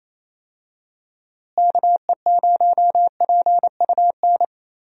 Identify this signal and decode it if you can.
Morse code, heard as a keyed tone, about 28 wpm, 700 Hz. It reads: KE0PUN
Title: USA Call Signs - 6 Characters - Repeated 2x - 28wpm